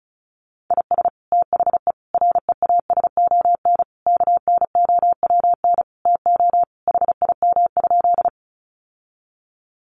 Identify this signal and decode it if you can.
Morse code, heard as a keyed tone, about 34 wpm, 700 Hz. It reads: IS THE REASON KNOWN TO HIM?
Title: Sentences from Top 700 Words 35wpm